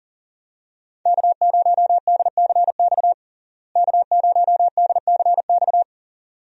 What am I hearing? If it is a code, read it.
Morse code, heard as a keyed tone, about 40 wpm, 700 Hz. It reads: K0DCX K0DCX
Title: USA Call Signs - 5 Characters - Repeated 2x - 40wpm